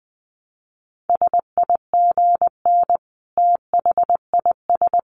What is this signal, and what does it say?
Morse code, heard as a keyed tone, about 20 wpm, 700 Hz. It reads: SIGN THIS
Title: A Sound Of Thunder 20wpm